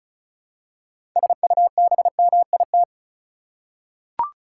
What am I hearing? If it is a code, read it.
Morse code, heard as a keyed tone, about 34 wpm, 700 Hz. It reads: SUBMIT
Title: Six Letter Words - 35wpm